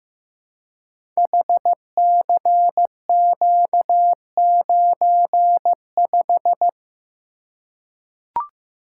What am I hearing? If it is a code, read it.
Morse code, heard as a keyed tone, about 15 wpm, 700 Hz. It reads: HCQ95
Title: Code Groups - 5 Letter:Numbers - 15wpm